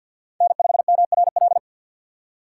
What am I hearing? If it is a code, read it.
Morse code, heard as a keyed tone, about 49 wpm, 700 Hz. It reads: N5MRL
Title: USA Call Signs - 5 Characters - 3x Character Spacing - 50wpm